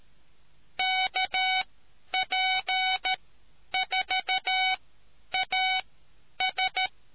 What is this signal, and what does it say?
Morse code, heard as a keyed tone, about 13 wpm, spaced as for 9 wpm, 750 Hz. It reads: KP4AS